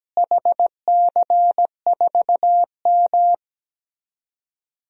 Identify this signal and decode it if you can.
Morse code, heard as a keyed tone, about 17 wpm, 700 Hz. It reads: HC4M
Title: DXCC-100 Call Signs - 4 Characters - Repeated 3x - 17wpm